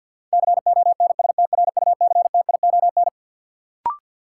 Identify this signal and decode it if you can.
Morse code, heard as a keyed tone, about 53 wpm, 700 Hz. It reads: CONSTRUCTION
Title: Sentences from Top 1600 Words - New Word Drill Alt - 50wpm